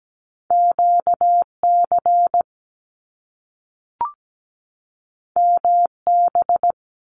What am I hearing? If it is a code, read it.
Morse code, heard as a keyed tone, about 17 wpm, 700 Hz. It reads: QC MB
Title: Canadian Provinces and Territories 17wpm